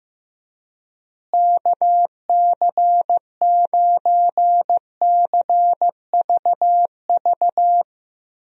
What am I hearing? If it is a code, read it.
Morse code, heard as a keyed tone, about 15 wpm, 700 Hz. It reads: KC9CVV
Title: USA Call Signs - 6 Characters - Repeated 3x - 15wpm